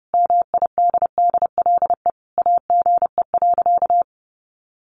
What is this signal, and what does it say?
Morse code, heard as a keyed tone, about 30 wpm, 700 Hz. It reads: MIDDLE AGE.
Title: Sentences from Top 1400 Words 30wpm